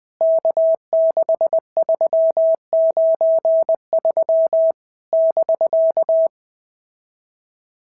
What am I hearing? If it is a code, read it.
Morse code, heard as a keyed tone, about 20 wpm, 650 Hz. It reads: K6393 <BK>
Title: POTA QSO Practice - 20wpm